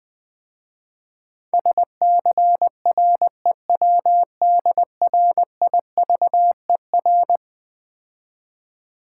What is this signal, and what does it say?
Morse code, heard as a keyed tone, about 20 wpm, 700 Hz. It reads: SCREWDRIVER
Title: Easy Words - Speed-Racing with Hints - 20wpm